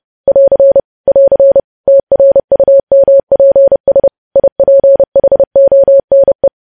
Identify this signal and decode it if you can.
Morse code, heard as a keyed tone, about 30 wpm, 550 Hz. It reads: <AR> <AR> TRUMPS IPHONE
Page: news headlines in morse code at 30 wpm